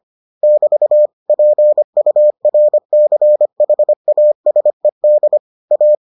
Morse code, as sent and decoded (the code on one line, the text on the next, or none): -...- / .--. ..- .-. -.-. .... .- ... -.. / .-
<BT> PURCHASED A